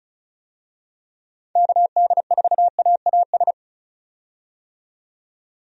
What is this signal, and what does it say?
Morse code, heard as a keyed tone, about 36 wpm, 700 Hz. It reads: KD4AAS
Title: USA Call Signs - 6 Characters - 35wpm